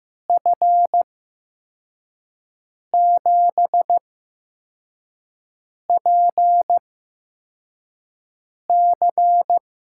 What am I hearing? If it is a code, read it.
Morse code, heard as a keyed tone, about 15 wpm, spaced as for 4 wpm, 700 Hz. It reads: F7PC
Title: USA Call Signs - 6 Characters - 8x Character Spacing - 15wpm